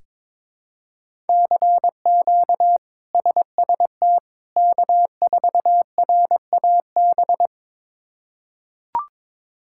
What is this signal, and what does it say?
Morse code, heard as a keyed tone, about 22 wpm, 700 Hz. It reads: CQ SST K4RAB
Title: K1USN SST QSO Practice - 22wpm